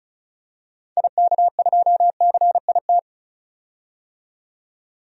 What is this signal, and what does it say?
Morse code, heard as a keyed tone, about 34 wpm, 700 Hz. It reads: IK2CIT
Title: DXCC-100 Call Signs - 6 Characters - 35wpm